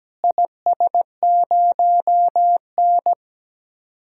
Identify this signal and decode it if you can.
Morse code, heard as a keyed tone, about 17 wpm, 700 Hz. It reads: IS0N
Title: DXCC-100 Call Signs - 4 Characters - Repeated 2x - 17wpm